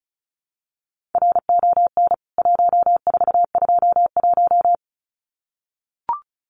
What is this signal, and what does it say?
Morse code, heard as a keyed tone, about 34 wpm, 700 Hz. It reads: RON 1421
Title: CWT Exchange 35wpm